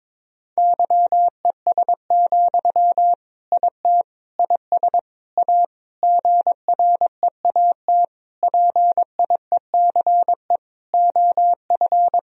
Morse code, heard as a keyed tone, about 22 wpm, 700 Hz. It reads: YES, IT IS A GREAT PIECE OF
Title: Sentences from Top 1000 Words - Review Part 2 - 22wpm